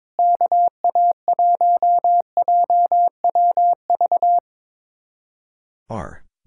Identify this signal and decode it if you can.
Morse code, heard as a keyed tone, about 22 wpm, 700 Hz. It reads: KA1JWV
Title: Speed-Racing Common QSO Phrases - 22wpm